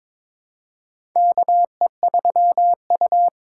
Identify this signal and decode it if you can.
Morse code, heard as a keyed tone, about 22 wpm, 700 Hz. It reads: KE3U
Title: USA Call Signs - 4 Characters - 4x Character Spacing - 22wpm